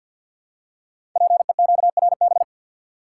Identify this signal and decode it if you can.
Morse code, heard as a keyed tone, about 52 wpm, 700 Hz. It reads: PEQRB
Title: Code Groups - 5 Letters - 50wpm